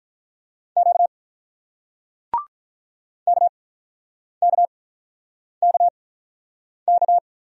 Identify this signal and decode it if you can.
Morse code, heard as a keyed tone, about 46 wpm, spaced as for 9 wpm, 690 Hz. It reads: Q KKKK